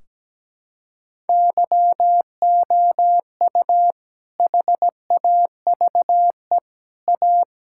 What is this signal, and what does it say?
Morse code, heard as a keyed tone, about 17 wpm, 700 Hz. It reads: YOU HAVE A
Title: Sentences from Top 1600 Words 17wpm